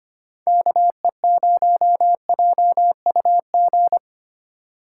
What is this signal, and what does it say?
Morse code, heard as a keyed tone, about 25 wpm, 700 Hz. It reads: KE0JUG
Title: USA Call Signs - 6 Characters - 6x Character Spacing - 25wpm